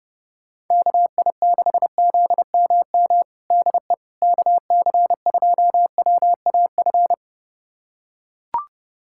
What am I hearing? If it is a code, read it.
Morse code, heard as a keyed tone, about 32 wpm, 700 Hz. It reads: KI6ZMM DE KC2WAF
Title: USA Callsign DE Callsign 30wpm